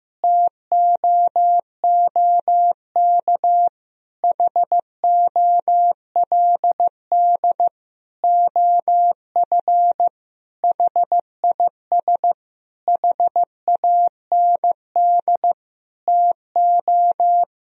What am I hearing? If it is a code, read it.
Morse code, heard as a keyed tone, about 15 wpm, 700 Hz. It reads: TOOK HOLD OF HIS HAND TO
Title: Sentences from Top 1000 Words - Review Part 4 - 15wpm